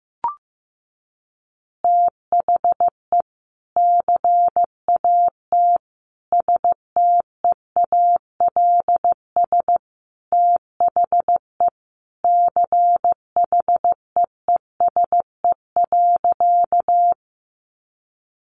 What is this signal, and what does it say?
Morse code, heard as a keyed tone, about 15 wpm, 700 Hz. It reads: THE CAT STEALS THE CHEESE.
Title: Easy Phrases 15wpm